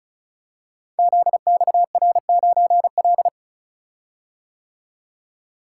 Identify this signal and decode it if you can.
Morse code, heard as a keyed tone, about 36 wpm, 700 Hz. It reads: ZXR9L
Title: Code Groups - 5 Letter:Numbers - 35wpm